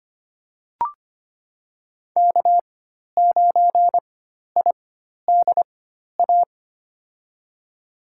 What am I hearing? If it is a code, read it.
Morse code, heard as a keyed tone, about 25 wpm, spaced as for 12 wpm, 700 Hz. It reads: K9IDA
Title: USA Call Signs - 5 Characters - 4x Character Spacing - 25wpm